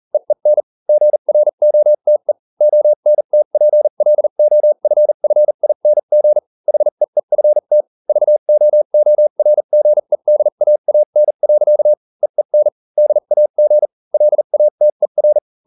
Morse code, heard as a keyed tone, about 40 wpm, spaced as for 35 wpm, 600 Hz. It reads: EEN GROTE ONTPLOFFING HEEFT VOORGEDAAN. EEN DAG LATER